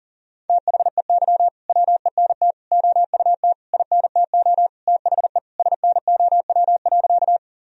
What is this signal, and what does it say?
Morse code, heard as a keyed tone, about 40 wpm, 700 Hz. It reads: THEY WENT OUT INTO THE SNOW.
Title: Sentences from Top 1000 Words - Review Part 3 - 40wpm